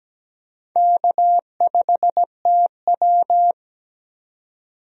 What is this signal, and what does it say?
Morse code, heard as a keyed tone, about 17 wpm, 700 Hz. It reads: K5TW
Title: USA Call Signs - 4 Characters - Repeated 3x - 17wpm